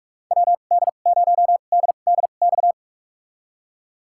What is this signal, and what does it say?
Morse code, heard as a keyed tone, about 46 wpm, spaced as for 32 wpm, 700 Hz. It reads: WD0DDX
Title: USA Call Signs - 6 Characters - 2x Character Spacing - 45wpm